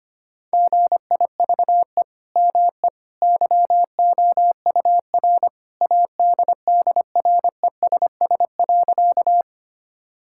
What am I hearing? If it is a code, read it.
Morse code, heard as a keyed tone, about 25 wpm, 700 Hz. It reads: GIVE ME YOUR ADDRESS.
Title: Sentences from Top 1000 Words - Review Part 6 - 25wpm